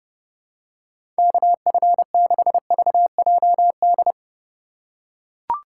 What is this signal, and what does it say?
Morse code, heard as a keyed tone, about 32 wpm, 700 Hz. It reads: KF6VJD
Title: USA Call Signs - 6 Characters - 2x Character Spacing - 30wpm